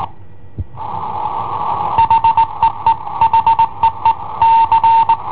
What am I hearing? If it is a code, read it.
Morse code, heard as a keyed tone, about 20 wpm, 910 Hz. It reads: HEE HEE C
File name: i7ohp / cassetta i7ohp B